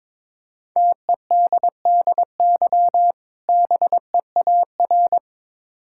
Morse code, heard as a keyed tone, about 22 wpm, 690 Hz. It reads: TEDDY BEAR
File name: Difficult Words with Hints - 22wpm